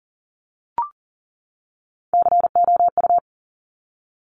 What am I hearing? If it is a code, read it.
Morse code, heard as a keyed tone, about 38 wpm, 700 Hz. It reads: COU
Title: Top Three Letter Combinations in Words 40wpm